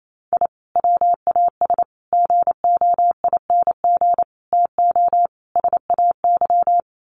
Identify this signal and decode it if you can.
Morse code, heard as a keyed tone, about 28 wpm, 700 Hz. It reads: I WAS GOING TO SAY
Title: Sets of 5 Words 28wpm